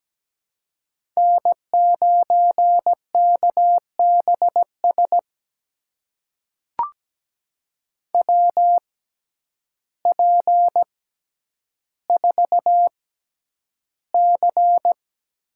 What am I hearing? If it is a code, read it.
Morse code, heard as a keyed tone, about 17 wpm, spaced as for 9 wpm, 700 Hz. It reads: N9KBS WP4C
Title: USA Call Signs - 5 Characters - 6x Character Spacing - 17wpm